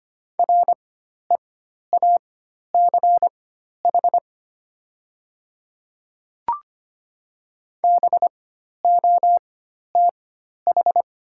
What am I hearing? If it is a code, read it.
Morse code, heard as a keyed tone, about 25 wpm, spaced as for 12 wpm, 690 Hz. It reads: REACH BOTH
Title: 4x Character Spacing - Top 500 Words - 25wpm